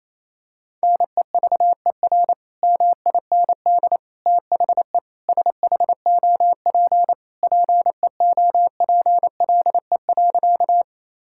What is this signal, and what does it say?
Morse code, heard as a keyed tone, about 28 wpm, 700 Hz. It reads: NEVER MIND THE SHOP PEOPLE.
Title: Sentences from Top 1600 Words 28wpm